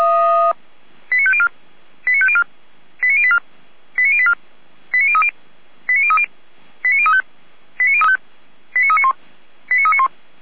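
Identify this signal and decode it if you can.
Five-tone selective call (ZVEI), heard as quick runs of five tones, tones of about 70 ms. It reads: 89504 89504 89084 89084 89030 89030 89035 89391 89391
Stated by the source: Five-tone ZVEI selective call received by SDR, fire brigade Orlamünde (feuerwehr-orlamuende).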